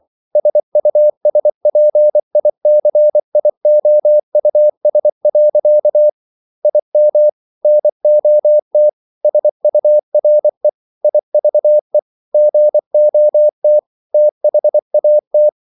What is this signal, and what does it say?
Morse code, heard as a keyed tone, about 24 wpm, 600 Hz. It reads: SUSPICIOUS. IM NOT SURE IVE GOT THAT